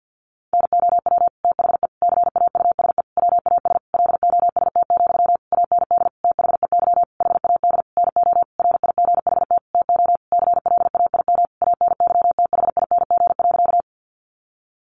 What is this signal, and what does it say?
Morse code, heard as a keyed tone, about 50 wpm, 700 Hz. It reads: NOW THE CAUSE WAS LOST, AND THEY HAD NO RIGHT TO CLAIM ANYTHING.
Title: Sentences from Top 1500 Words 50wpm